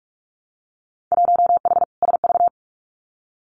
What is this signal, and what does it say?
Morse code, heard as a keyed tone, about 46 wpm, 700 Hz. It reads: 1H SV
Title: Winter Field Day Exchanges 45wpm